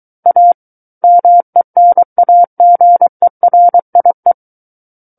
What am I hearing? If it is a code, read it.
Morse code, heard as a keyed tone, about 23 wpm, 700 Hz. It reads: A MENAGERIE
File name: Book One / 14-Chapter 14 - 23 WPM